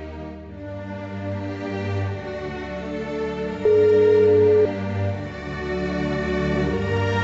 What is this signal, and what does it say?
Ringback tone (Europe, 425 Hz), with noise.